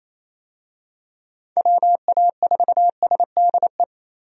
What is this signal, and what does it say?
Morse code, heard as a keyed tone, about 28 wpm, 700 Hz. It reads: WA4SDE
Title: USA Call Signs - 6 Characters - 28wpm